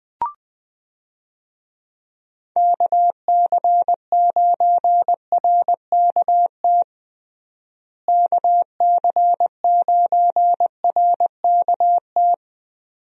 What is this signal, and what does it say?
Morse code, heard as a keyed tone, about 20 wpm, 700 Hz. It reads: KC9RKT KC9RKT
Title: USA Call Signs - 6 Characters - Repeated 2x - 20wpm